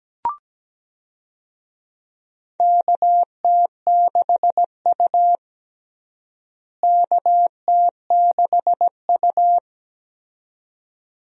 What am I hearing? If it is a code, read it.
Morse code, heard as a keyed tone, about 17 wpm, 700 Hz. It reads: KT6U KT6U
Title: USA Call Signs - 4 Characters - Repeated 2x - 17wpm